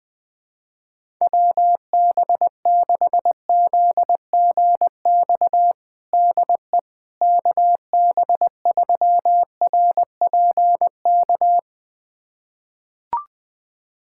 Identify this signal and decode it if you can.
Morse code, heard as a keyed tone, about 20 wpm, 700 Hz. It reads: WB6ZGX DE KB3RPK
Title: USA Callsign DE Callsign 20wpm